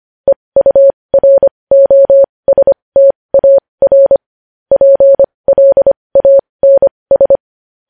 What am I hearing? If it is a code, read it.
Morse code, heard as a keyed tone, about 25 wpm, spaced as for 20 wpm, 550 Hz. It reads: EUROSTAR PLANS
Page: News Headlines in Morse Code at 20 WPM